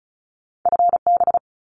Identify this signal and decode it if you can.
Morse code, heard as a keyed tone, about 34 wpm, 700 Hz. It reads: FB
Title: Abbreviations Repeated 3x - 35wpm